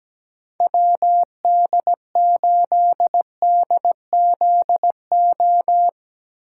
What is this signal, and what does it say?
Morse code, heard as a keyed tone, about 17 wpm, 700 Hz. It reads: WD8DZO
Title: USA Call Signs - 6 Characters - 8x Character Spacing - 17wpm